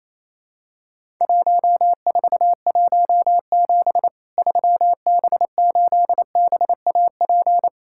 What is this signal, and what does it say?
Morse code, heard as a keyed tone, about 27 wpm, 700 Hz. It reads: 1417 3B8BAP
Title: Worked All Europe DX Contest - QTC Exchange - 28wpm